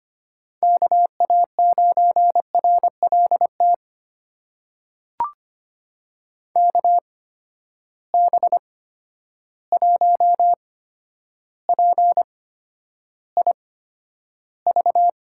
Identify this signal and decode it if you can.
Morse code, heard as a keyed tone, about 25 wpm, spaced as for 11 wpm, 700 Hz. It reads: KA9RLT KB1PIV
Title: USA Call Signs - 6 Characters - 8x Character Spacing - 25wpm